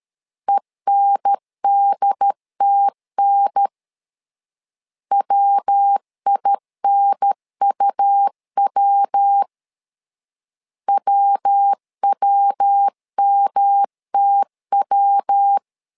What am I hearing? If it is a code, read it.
Morse code, heard as a keyed tone, about 13 wpm, 800 Hz. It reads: ENDTN WINUW WWMTW
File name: LEZIONI / 3 MWRO / 3 MW 12wpm 1633